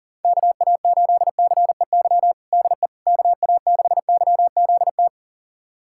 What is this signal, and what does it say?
Morse code, heard as a keyed tone, about 40 wpm, 700 Hz. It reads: KA8CEY DE KA6YZT
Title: USA Callsign DE Callsign - 6 Characters - 40wpm